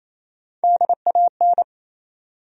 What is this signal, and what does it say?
Morse code, heard as a keyed tone, about 28 wpm, 700 Hz. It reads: DAN
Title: USA Common Names - Max 5 Letters - 28wpm